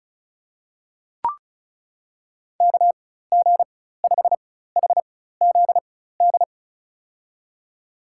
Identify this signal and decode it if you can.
Morse code, heard as a keyed tone, about 36 wpm, spaced as for 16 wpm, 700 Hz. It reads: KG5HZD
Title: USA Call Signs - 6 Characters - 4x Character Spacing - 35wpm